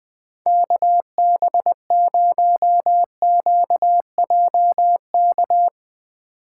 Morse code, heard as a keyed tone, about 20 wpm, 700 Hz. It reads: KB0QJK
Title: Speed Racing - USA 2x3 Call Signs 17wpm